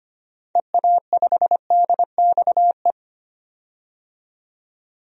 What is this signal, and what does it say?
Morse code, heard as a keyed tone, about 25 wpm, 700 Hz. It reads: EA5DXE
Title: DXCC-100 Call Signs - 6 Characters - 25wpm